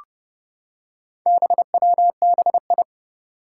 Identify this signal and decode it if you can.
Morse code, heard as a keyed tone, about 31 wpm, 700 Hz. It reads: BWBI
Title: Four Challenging Letter-Number Combinations 30wpm